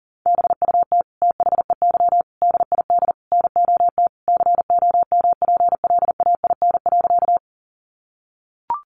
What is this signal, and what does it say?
Morse code, heard as a keyed tone, about 40 wpm, 700 Hz. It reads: BUT THEY DID NOT COMPLAIN.
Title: Sentences from Top 1300 Words 40wpm